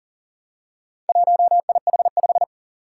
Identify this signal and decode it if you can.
Morse code, heard as a keyed tone, about 38 wpm, 700 Hz. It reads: 1IH5
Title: Four Challenging Letter-Number Combinations 40wpm